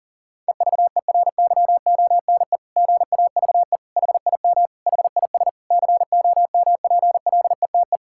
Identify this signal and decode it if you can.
Morse code, heard as a keyed tone, about 40 wpm, 700 Hz. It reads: EVERYONE GAVE HIM HIS COMPLETE
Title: Sentences from Top 1000 Words - Review Part 6 - 40wpm